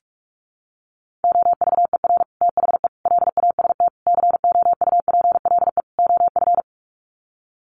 Morse code, heard as a keyed tone, about 44 wpm, 700 Hz. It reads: OVER THE LAST COUPLE OF
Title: Sets of 5 Words Encore 45wpm